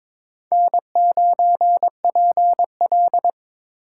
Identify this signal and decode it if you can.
Morse code, heard as a keyed tone, about 22 wpm, 700 Hz. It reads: N9PL
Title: USA Call Signs - 4 Characters - 4x Character Spacing - 22wpm